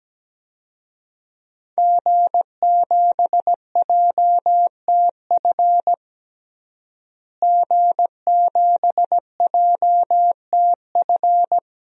Morse code, heard as a keyed tone, about 17 wpm, 700 Hz. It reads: G7JTF G7JTF
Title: DXCC-100 Call Signs - 5 Characters - Repeated 3x - 17wpm